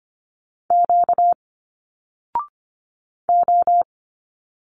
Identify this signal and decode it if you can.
Morse code, heard as a keyed tone, about 25 wpm, 700 Hz. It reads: Q O
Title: Single Letters 25wpm